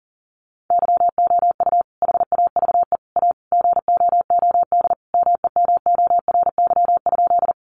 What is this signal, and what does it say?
Morse code, heard as a keyed tone, about 40 wpm, 700 Hz. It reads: YOU HAVE A GOOD MEMORY?
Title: Sentences from Top 1300 Words 40wpm